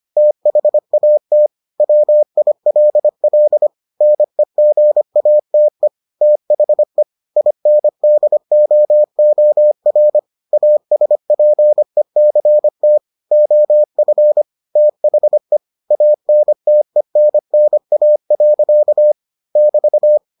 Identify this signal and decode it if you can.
Morse code, heard as a keyed tone, about 25 wpm, 600 Hz. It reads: THAT WILL NEGATE THE INDOOR ASPECT OF THE ANTENNA. <BT>